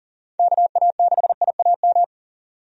Morse code, heard as a keyed tone, about 40 wpm, 700 Hz. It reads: KA6IAM
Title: USA Call Signs - 6 Characters - 40wpm